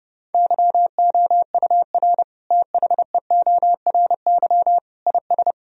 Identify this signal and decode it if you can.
Morse code, heard as a keyed tone, about 32 wpm, 700 Hz. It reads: YOUR THEORY IS
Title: Sentences from Top 1000 Words - Review Part 3 - 30wpm